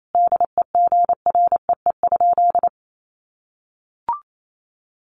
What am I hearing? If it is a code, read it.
Morse code, heard as a keyed tone, about 28 wpm, 700 Hz. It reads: DEGREE?
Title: Sentences from Top 1500 Words 28wpm